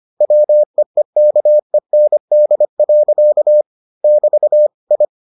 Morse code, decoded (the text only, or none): WEEKEND. <BT> I